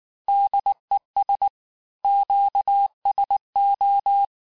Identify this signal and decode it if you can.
Morse code, heard as a keyed tone, about 19 wpm, 800 Hz. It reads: DES QSO
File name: cours-du-lundi files / cours-1583-fin-QSO-18-mpm